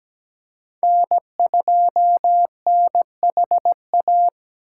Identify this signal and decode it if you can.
Morse code, heard as a keyed tone, about 17 wpm, 700 Hz. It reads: N2NHA
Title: USA Call Signs - 5 Characters - 2x Character Spacing - 17wpm